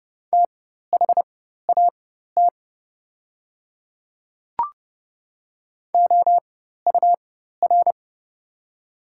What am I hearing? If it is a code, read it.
Morse code, heard as a keyed tone, about 32 wpm, spaced as for 14 wpm, 700 Hz. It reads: THAT OUR
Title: 4x Character Spacing - Top 200 Words - 30wpm